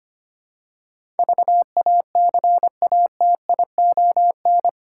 Morse code, heard as a keyed tone, about 25 wpm, 700 Hz. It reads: VACATION
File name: Eight Letter Words - 25wpm